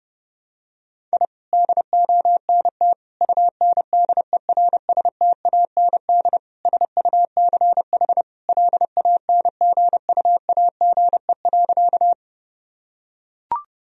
Morse code, heard as a keyed tone, about 32 wpm, 700 Hz. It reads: I DONT UNDERSTAND SUCH LANGUAGE.
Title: Sentences from Top 1000 Words - Review Part 3 - 30wpm